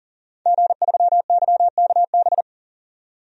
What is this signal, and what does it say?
Morse code, heard as a keyed tone, about 37 wpm, 700 Hz. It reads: G3YKB
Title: DXCC-100 Call Signs - 5 Characters - 40wpm